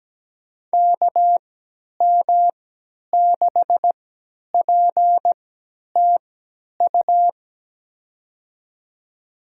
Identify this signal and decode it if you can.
Morse code, heard as a keyed tone, about 17 wpm, spaced as for 10 wpm, 700 Hz. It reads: KM6PTU
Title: USA Call Signs - 6 Characters - 3x Character Spacing - 17wpm